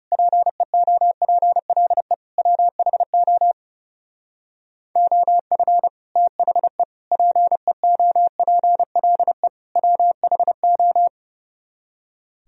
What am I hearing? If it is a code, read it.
Morse code, heard as a keyed tone, about 35 wpm, 700 Hz. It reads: PEOPLE WHO OF THE PEOPLE WHO
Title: Speed-Racing - Sets of 4 Words 30wpm